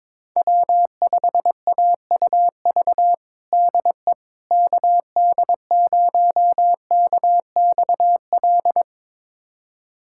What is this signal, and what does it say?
Morse code, heard as a keyed tone, about 22 wpm, 700 Hz. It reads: W5AUV DE KD0KXL
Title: USA Callsign DE Callsign 22wpm